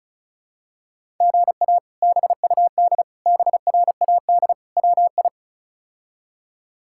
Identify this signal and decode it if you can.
Morse code, heard as a keyed tone, about 34 wpm, 700 Hz. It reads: GA BUD BRAD WI
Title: K1USN SST QSO Practice - 35wpm